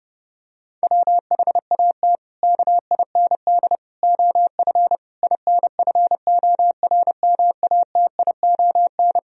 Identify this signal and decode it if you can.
Morse code, heard as a keyed tone, about 32 wpm, 700 Hz. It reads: WHAT KIND OF INFORMATION
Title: Sentences from Top 1000 Words - Review Part 3 - 30wpm